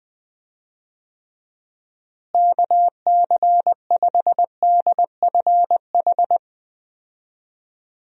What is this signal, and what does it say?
Morse code, heard as a keyed tone, about 20 wpm, 700 Hz. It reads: KC5DFH
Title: USA Call Signs - 6 Characters - Repeated 3x - 20wpm